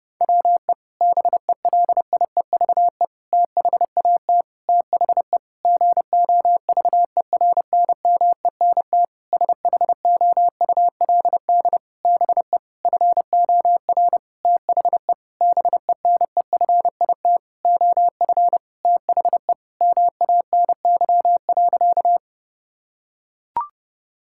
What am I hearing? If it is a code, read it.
Morse code, heard as a keyed tone, about 30 wpm, 700 Hz. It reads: WE BELIEVE THAT THE GOVERNMENT SHOULD BE FOR THE BENEFIT OF THE MANY.
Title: Sentences from Top 1000 Words - Review Part 6 - 30wpm